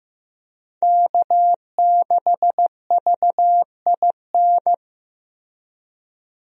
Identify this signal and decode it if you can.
Morse code, heard as a keyed tone, about 15 wpm, 700 Hz. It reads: K6VIN
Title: USA Call Signs - 5 Characters - Repeated 3x - 15wpm